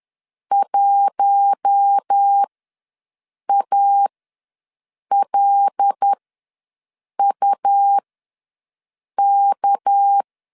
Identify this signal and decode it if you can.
Morse code, heard as a keyed tone, about 10 wpm, spaced as for 6 wpm, 800 Hz. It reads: 1ALUK
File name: LEZIONI / 6 PKX0 / 6 PK ws 10wpm 3745